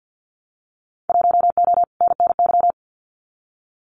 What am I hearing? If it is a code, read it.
Morse code, heard as a keyed tone, about 48 wpm, 700 Hz. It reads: 1O NNY
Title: Winter Field Day Exchanges 50wpm